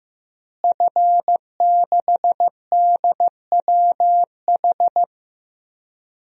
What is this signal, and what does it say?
Morse code, heard as a keyed tone, about 16 wpm, 700 Hz. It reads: F6DWH
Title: DXCC-100 Call Signs - 5 Characters - Repeated 2x - 15wpm